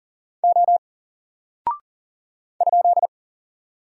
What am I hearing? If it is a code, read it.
Morse code, heard as a keyed tone, about 40 wpm, 700 Hz. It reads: O ?
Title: Single Character (Letter:Number:Punctuation:Prosign) Alt 40wpm